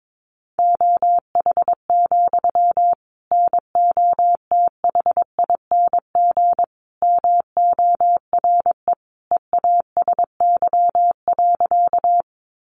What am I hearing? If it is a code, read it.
Morse code, heard as a keyed tone, about 22 wpm, 700 Hz. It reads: OH, NOTHING MORE EASY.
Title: Sentences from Top 700 Words 22wpm